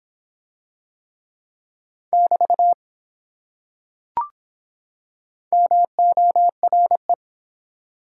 Morse code, heard as a keyed tone, about 24 wpm, 700 Hz. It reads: <BT> MORE